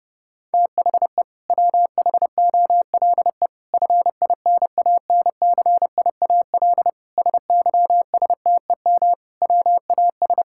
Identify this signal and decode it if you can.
Morse code, heard as a keyed tone, about 30 wpm, 700 Hz. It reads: THE WHOLE FINANCIAL SYSTEM WAS
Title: Sentences from Top 1000 Words - Review Part 5 - 30wpm